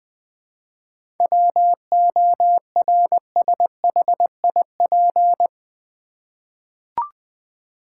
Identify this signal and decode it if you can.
Morse code, heard as a keyed tone, about 20 wpm, 700 Hz. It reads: WORSHIP
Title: 7 Letter Words - First 3 Letters Given - 20wpm